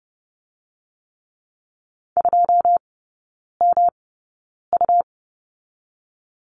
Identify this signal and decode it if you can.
Morse code, heard as a keyed tone, about 30 wpm, spaced as for 9 wpm, 700 Hz. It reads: 2MU
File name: Lesson 18e - Letter m - Three Characters 30wpm